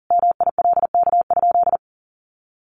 Morse code, heard as a keyed tone, about 40 wpm, 700 Hz. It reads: MILK?
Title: Sentences from Top 1500 Words 40wpm